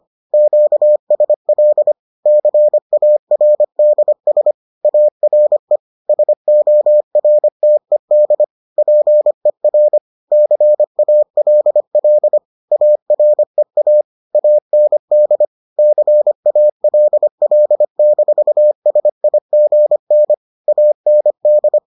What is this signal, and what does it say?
Morse code, heard as a keyed tone, about 25 wpm, 600 Hz. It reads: QSL CARDS ARE SORTED PER CALL AREA AND CALL-SIGN AND